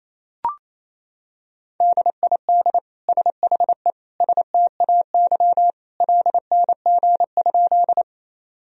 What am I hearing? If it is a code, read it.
Morse code, heard as a keyed tone, about 28 wpm, 700 Hz. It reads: DID SHE STAY LNG?
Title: Sentences from Top 500 Words and Abbreviations 28wpm